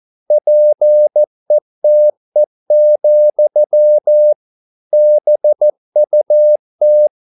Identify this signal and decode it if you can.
Morse code, heard as a keyed tone, about 14 wpm, 600 Hz. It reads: PETE, BUT